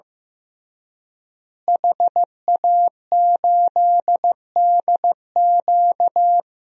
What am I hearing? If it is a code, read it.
Morse code, heard as a keyed tone, about 15 wpm, 700 Hz. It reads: HA8DQ
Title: DXCC-100 Call Signs - 5 Characters - Repeated 3x - 15wpm